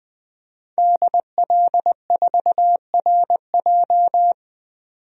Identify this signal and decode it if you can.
Morse code, heard as a keyed tone, about 21 wpm, 700 Hz. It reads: DL4RJ
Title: DXCC-100 Call Signs - 5 Characters - 20wpm